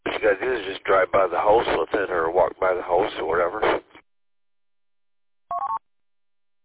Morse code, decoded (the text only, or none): none